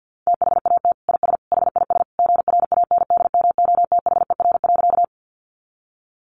Morse code, heard as a keyed tone, about 50 wpm, 700 Hz. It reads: THAT IS HIS GRANDMOTHER.
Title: Sentences from Top 1200 Words 50wpm